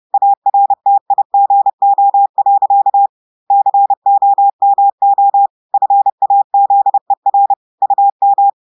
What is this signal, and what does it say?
Morse code, heard as a keyed tone, about 30 wpm, 800 Hz. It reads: ARTIGO. COMO FAZER UM
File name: audios / portugais / portugais texte en clair 2 vitesse 30 mots minutes